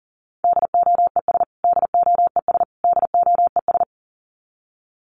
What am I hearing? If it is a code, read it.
Morse code, heard as a keyed tone, about 40 wpm, 700 Hz. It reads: DOES DOES DOES
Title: Top 200 Words Repeated 40wpm